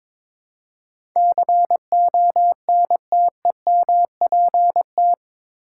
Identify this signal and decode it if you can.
Morse code, heard as a keyed tone, about 22 wpm, 700 Hz. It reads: CONTEMPT
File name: practice / 22wpm / Eight Letter Words - 22wpm